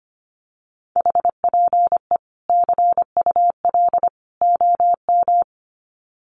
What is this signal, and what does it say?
Morse code, heard as a keyed tone, about 25 wpm, 700 Hz. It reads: HPE CUL OM
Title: Common QSO Phrases 25wpm